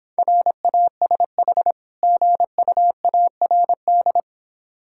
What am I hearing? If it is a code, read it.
Morse code, heard as a keyed tone, about 26 wpm, 690 Hz. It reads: RASH GUARD